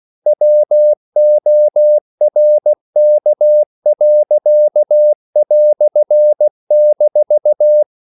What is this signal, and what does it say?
Morse code, heard as a keyed tone, about 16 wpm, 600 Hz. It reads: WORK."-
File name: CW / Practice / EnglishQuotations0216